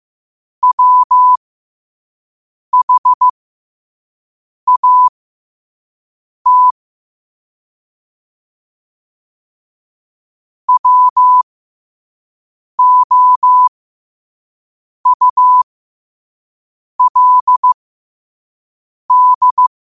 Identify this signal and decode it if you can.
Morse code, heard as a keyed tone, about 15 wpm, spaced as for 5 wpm, 1000 Hz. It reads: WHAT WOULD
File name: qotdfiles / 5wpm